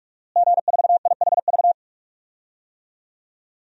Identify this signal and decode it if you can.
Morse code, heard as a keyed tone, about 43 wpm, 700 Hz. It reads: G4IHV
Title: DXCC-100 Call Signs - 5 Characters - Repeated 2x - 45wpm